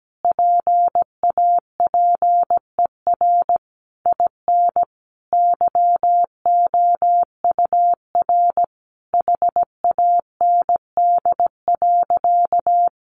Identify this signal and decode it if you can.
Morse code, heard as a keyed tone, about 17 wpm, 700 Hz. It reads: PAPER IN YOUR HAND.
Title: Sentences from Top 400 Words 17wpm